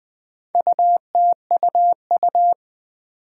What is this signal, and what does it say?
Morse code, heard as a keyed tone, about 20 wpm, 700 Hz. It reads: UTUU
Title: Four Challenging Letter-Number Combinations 20wpm